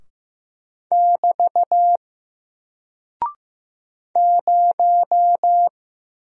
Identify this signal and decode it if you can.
Morse code, heard as a keyed tone, about 15 wpm, 700 Hz. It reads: <BT> 0
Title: Single Character (Letter:Number:Punctuation:Prosign) 15wpm